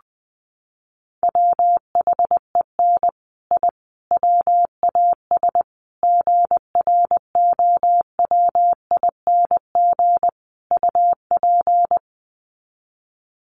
Morse code, heard as a keyed tone, about 20 wpm, 700 Hz. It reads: WHEN I WAS GROWING UP
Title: Sets of 5 Words Encore 20wpm